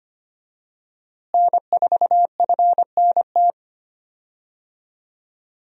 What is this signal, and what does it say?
Morse code, heard as a keyed tone, about 25 wpm, 700 Hz. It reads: N4FNT
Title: USA Call Signs - 5 Characters - Repeated 2x - 25wpm